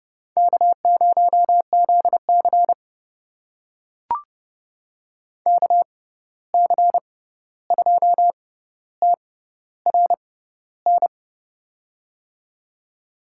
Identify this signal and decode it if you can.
Morse code, heard as a keyed tone, about 29 wpm, spaced as for 14 wpm, 700 Hz. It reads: K0ZC KC2TRN
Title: USA Call Signs - 6x Character Spacing - 30wpm